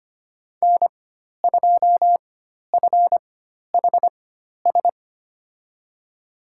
Morse code, heard as a keyed tone, about 25 wpm, spaced as for 12 wpm, 700 Hz. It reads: N2FHS
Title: USA Call Signs - 5 Characters - 4x Character Spacing - 25wpm